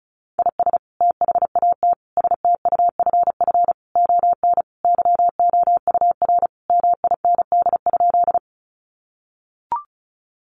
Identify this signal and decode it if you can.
Morse code, heard as a keyed tone, about 34 wpm, 700 Hz. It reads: IS THAT STUFF ON YOUR MIND?
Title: Sentences from Top 1400 Words 35wpm